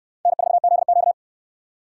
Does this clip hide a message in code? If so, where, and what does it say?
Yes: Morse code from 0.3 to 1.1 s, heard as a keyed tone, about 61 wpm, 700 Hz. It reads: N4ZQ